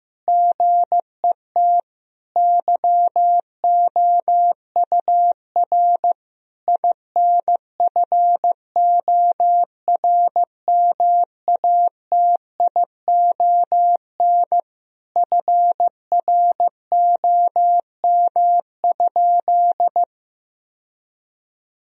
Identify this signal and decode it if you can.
Morse code, heard as a keyed tone, about 16 wpm, 700 Hz. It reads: GET YOUR INFORMATION FROM?
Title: Sentences from Top 1000 Words - Review Part 3 - 15wpm